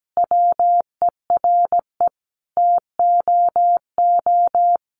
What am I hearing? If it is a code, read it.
Morse code, heard as a keyed tone, about 17 wpm, 700 Hz. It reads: WERE TOO
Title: Sentences from Top 1500 Words 17wpm